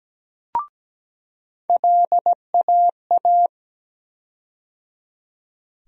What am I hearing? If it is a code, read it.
Morse code, heard as a keyed tone, about 17 wpm, 700 Hz. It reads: LAA
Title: USA Call Sign Postfixes - 3 Letters 17wpm